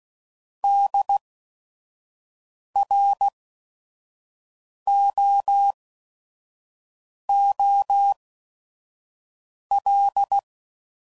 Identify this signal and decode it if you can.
Morse code, heard as a keyed tone, about 16 wpm, spaced as for 5 wpm, 800 Hz. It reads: DROOL
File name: Prefixes / By First Letter / Ds